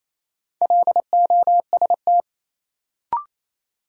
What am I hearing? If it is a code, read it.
Morse code, heard as a keyed tone, about 28 wpm, 700 Hz. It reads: LOST
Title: Sentences from Top 1400 Words - New Word Drill - 28wpm